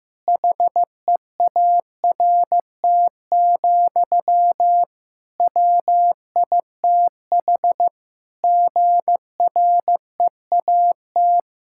Morse code, heard as a keyed tone, about 15 wpm, 700 Hz. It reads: HEART, WITH GREAT
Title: Sentences from Top 1600 Words 15wpm